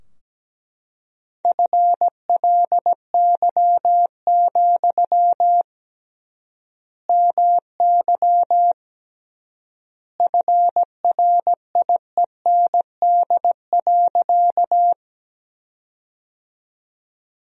Morse code, heard as a keyed tone, about 17 wpm, 700 Hz. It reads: FLY, MY FRIEND.
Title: 3x Word-Spacing - Sentences from Top 500 Words - 17wpm